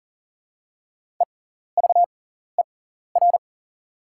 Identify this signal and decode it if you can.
Morse code, heard as a keyed tone, about 40 wpm, spaced as for 14 wpm, 690 Hz. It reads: EVER
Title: 6x Character Spacing - Top 500 Words - 40wpm